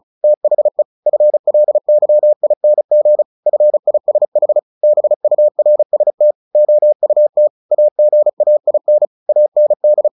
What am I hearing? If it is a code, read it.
Morse code, heard as a keyed tone, about 34 wpm, 600 Hz. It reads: THE FLYING FISH BURST OUT AGAIN AND
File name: the old man and the sea / the old man and the sea 1 35wpm